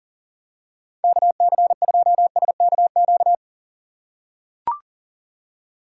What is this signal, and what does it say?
Morse code, heard as a keyed tone, about 39 wpm, 700 Hz. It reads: KC2SKQ
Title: USA Call Signs - 6 Characters - 3x Character Spacing - 40wpm